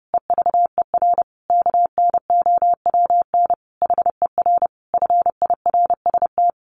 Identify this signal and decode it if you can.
Morse code, heard as a keyed tone, about 30 wpm, 700 Hz. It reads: EVER KNOWN HER FIRST
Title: Sentences from Top 1200 Words 30wpm